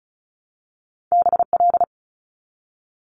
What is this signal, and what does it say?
Morse code, heard as a keyed tone, about 36 wpm, 700 Hz. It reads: BL
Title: Multi-letter Phonemes 35wpm